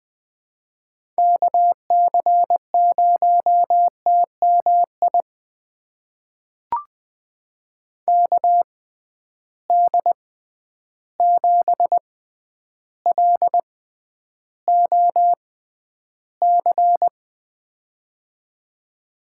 Morse code, heard as a keyed tone, about 20 wpm, spaced as for 10 wpm, 700 Hz. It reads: KC0TMI KD7LOC
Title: USA Call Signs - 6 Characters - 6x Character Spacing - 20wpm